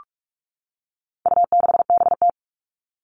Morse code, heard as a keyed tone, about 44 wpm, 700 Hz. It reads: U6BT
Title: All Cut Numbers - 4-Digit 45wpm